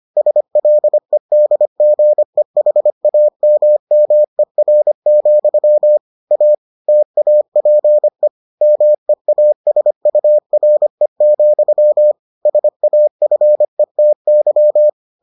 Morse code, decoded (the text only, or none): SLEDGEHAMMER, A TAPE MEASURE, SAFETY